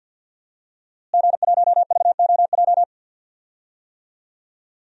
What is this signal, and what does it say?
Morse code, heard as a keyed tone, about 52 wpm, 700 Hz. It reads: G1VOJ
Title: DXCC-100 Call Signs - 5 Characters - Repeated 3x - 50wpm